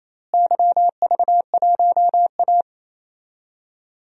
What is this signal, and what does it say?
Morse code, heard as a keyed tone, about 27 wpm, 700 Hz. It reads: YV1A
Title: DXCC-100 Call Signs - 4 Characters - Repeated 2x - 28wpm